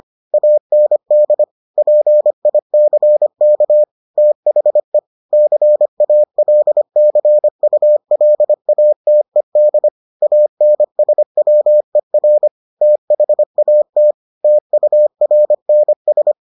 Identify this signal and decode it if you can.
Morse code, heard as a keyed tone, about 25 wpm, 600 Hz. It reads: AND PICK THE CALCULATED ANSWER THAT TURNS